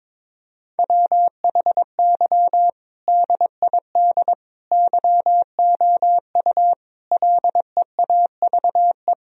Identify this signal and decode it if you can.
Morse code, heard as a keyed tone, about 22 wpm, 700 Hz. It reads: WHY DID YOU LEAVE
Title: Sentences from Top 1000 Words - Review Part 3 - 22wpm